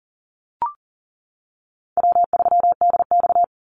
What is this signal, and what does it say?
Morse code, heard as a keyed tone, about 41 wpm, 700 Hz. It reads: W3DX
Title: NA SOTA Activator Callsigns 40wpm